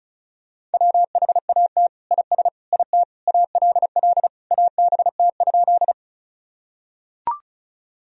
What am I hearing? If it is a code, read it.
Morse code, heard as a keyed tone, about 36 wpm, 700 Hz. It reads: WHAT IS IT ALL ABT?
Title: Sentences from Top 100 Words and Abbreviations 35wpm